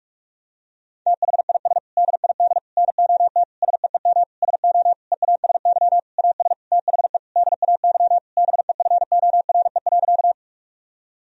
Morse code, heard as a keyed tone, about 41 wpm, 700 Hz. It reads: THIS DID NOT SEEM SO EASY AS THE DAY BEFORE.
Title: Sentences from Top 1000 Words - Review Part 1 - 45wpm